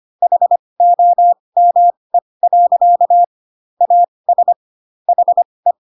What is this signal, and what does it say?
Morse code, heard as a keyed tone, about 25 wpm, spaced as for 20 wpm, 700 Hz. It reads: HOME. AS HE